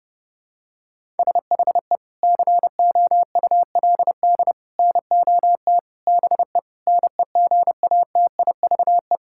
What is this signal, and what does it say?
Morse code, heard as a keyed tone, about 30 wpm, 700 Hz. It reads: SHE COULD NOT BE NEGATIVE
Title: Sentences from Top 1000 Words - Review Part 6 - 30wpm